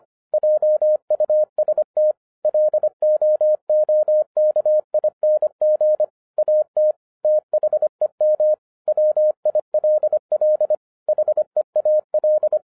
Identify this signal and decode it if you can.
Morse code, heard as a keyed tone, about 25 wpm, 610 Hz. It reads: JUST LOOKING AT THEM WILL HEAL
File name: 2011-07-mp3 / n24 QRP OTAKU 25wpm